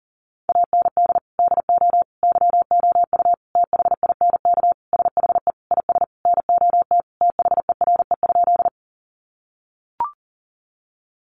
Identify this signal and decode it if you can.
Morse code, heard as a keyed tone, about 40 wpm, 700 Hz. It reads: AND DO YOU THINK SHE IS NOT THERE?
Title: Sentences from Top 600 Words 40wpm